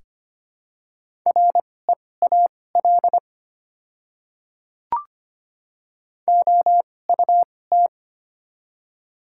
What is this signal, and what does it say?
Morse code, heard as a keyed tone, about 25 wpm, spaced as for 18 wpm, 700 Hz. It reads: REAL OUT
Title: 2x Character Spacing - Top 300 Words - 25wpm